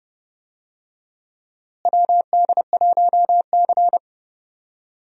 Morse code, heard as a keyed tone, about 30 wpm, 700 Hz. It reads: WD1C
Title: USA Call Signs - 4 Characters - Repeated 2x - 30wpm